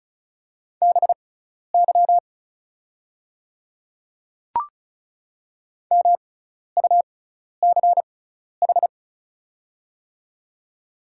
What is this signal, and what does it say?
Morse code, heard as a keyed tone, about 36 wpm, spaced as for 12 wpm, 700 Hz. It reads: BY MUCH
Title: 6x Character Spacing - Top 200 Words - 35wpm